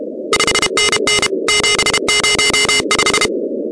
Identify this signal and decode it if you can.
Morse code, heard as a keyed tone, about 32 wpm, 2100 Hz, with noise. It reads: 5NN 705
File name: contests / 2019-cw-wpx-kt5j-k5pi- -2330 0-16